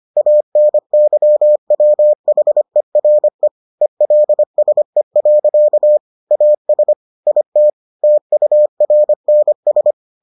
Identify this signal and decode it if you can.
Morse code, heard as a keyed tone, about 25 wpm, 600 Hz. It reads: ANYWHERE ELSE. AS IT TURNS